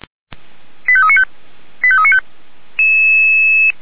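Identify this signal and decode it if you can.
Five-tone selective call (ZVEI), heard as quick runs of five tones, tones of about 70 ms. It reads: 85395 85395
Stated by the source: Five-tone ZVEI selective call received on an SDR, channel 464.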